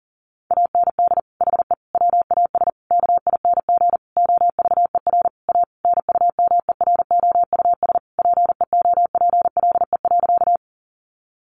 Morse code, heard as a keyed tone, about 40 wpm, 700 Hz. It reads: AND HE WAS KING OVER A NUMEROUS PEOPLE.
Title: Sentences from Top 1400 Words 40wpm